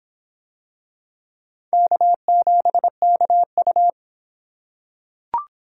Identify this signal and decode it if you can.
Morse code, heard as a keyed tone, about 26 wpm, 700 Hz. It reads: K7KU